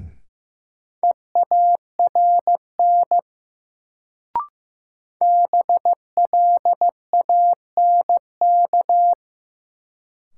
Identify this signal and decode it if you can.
Morse code, heard as a keyed tone, about 15 wpm, 700 Hz. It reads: EARN BLANK
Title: Sentences from Top 1500 Words - New Word Drill Alt - 15wpm